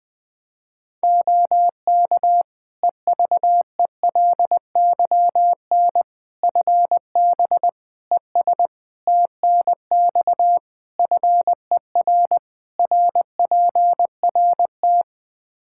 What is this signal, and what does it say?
Morse code, heard as a keyed tone, about 20 wpm, 700 Hz. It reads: OK EVELYN FB ES TNX FER RPRT
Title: Practice QSO - Exchange 2 - Info - 20wpm